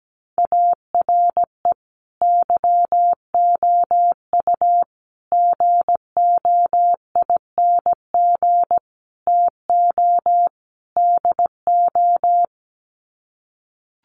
Morse code, heard as a keyed tone, about 17 wpm, 700 Hz. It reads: ARE YOU GOING TO DO
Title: Sets of 5 Words 17wpm